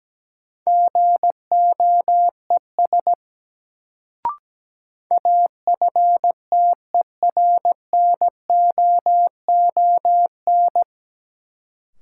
Morse code, heard as a keyed tone, about 17 wpm, 700 Hz. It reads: GOES AFTERNOON
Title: Sentences from Top 1500 Words - New Word Drill Alt - 17wpm